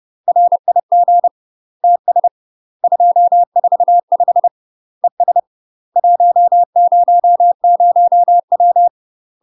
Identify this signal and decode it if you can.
Morse code, heard as a keyed tone, about 30 wpm, 700 Hz. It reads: RIG TS 245 ES 100W
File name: UK MASTER QSOs TRAINING / 30WPM / UK MASTER QSOs 0020